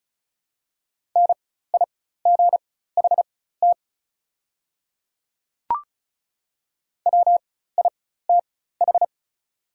Morse code, heard as a keyed tone, about 36 wpm, spaced as for 16 wpm, 700 Hz. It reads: NIGHT WITH
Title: 4x Character Spacing - Top 300 Words - 35wpm